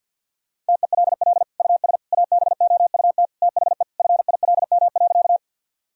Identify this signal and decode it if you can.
Morse code, heard as a keyed tone, about 51 wpm, 700 Hz. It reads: TELL US ABOUT THE FILM.
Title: Sentences from Top 1000 Words - Review Part 3 - 50wpm